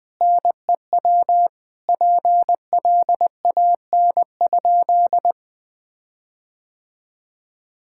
Morse code, heard as a keyed tone, about 20 wpm, 700 Hz. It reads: NEW PLAN?
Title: Sentences with 6-Word Phrases - Spoken Beginning - 20wpm